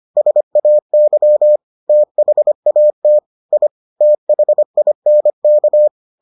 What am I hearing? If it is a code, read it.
Morse code, heard as a keyed tone, about 25 wpm, 600 Hz. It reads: SAY THAT I THINK